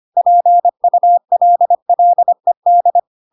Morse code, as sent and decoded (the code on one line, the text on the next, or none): .--. ..- .-.. .-.. -..
PULLED